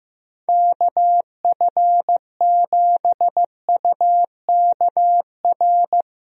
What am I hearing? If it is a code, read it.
Morse code, heard as a keyed tone, about 15 wpm, 700 Hz. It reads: KF7UKR
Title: USA Call Signs - 6 Characters - Repeated 2x - 15wpm